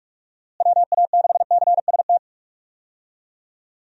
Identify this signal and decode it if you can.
Morse code, heard as a keyed tone, about 48 wpm, 700 Hz. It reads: WA6CST
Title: USA Call Signs - 6 Characters - Repeated 3x - 45wpm